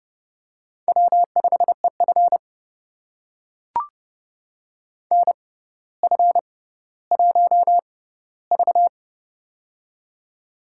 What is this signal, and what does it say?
Morse code, heard as a keyed tone, about 30 wpm, spaced as for 15 wpm, 700 Hz. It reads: W5EF NF1V
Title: USA Call Signs - 4 Characters - 6x Character Spacing - 30wpm